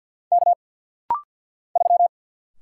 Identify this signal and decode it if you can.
Morse code, heard as a keyed tone, about 50 wpm, 700 Hz. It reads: K 3
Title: Single Letter-Number - Rapid-Fire 50wpm